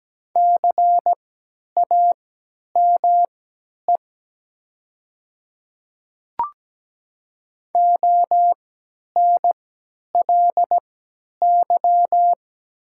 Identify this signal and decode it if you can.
Morse code, heard as a keyed tone, about 17 wpm, spaced as for 10 wpm, 700 Hz. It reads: CAME ONLY
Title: 3x Character Spacing - Top 200 Words - 17wpm